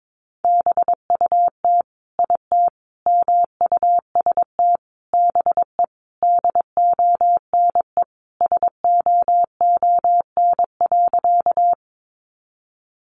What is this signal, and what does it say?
Morse code, heard as a keyed tone, about 22 wpm, 700 Hz. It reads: BUT IT MUST BE DONE SOON.
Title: Sentences from Top 500 Words 22wpm